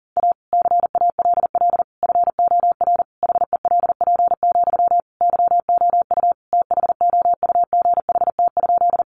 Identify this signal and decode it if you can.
Morse code, heard as a keyed tone, about 40 wpm, 700 Hz. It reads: A CALL FOR HELP, YOU THOUGHT?
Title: Sentences from Top 300 Words 40wpm